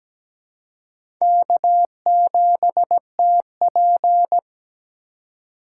Morse code, heard as a keyed tone, about 17 wpm, 700 Hz. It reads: K7TP
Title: USA Call Signs - 4 Characters - 4x Character Spacing - 17wpm